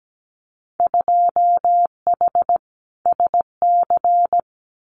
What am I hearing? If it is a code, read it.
Morse code, heard as a keyed tone, about 17 wpm, 700 Hz. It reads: 2H SC
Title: Winter Field Day Exchanges 17wpm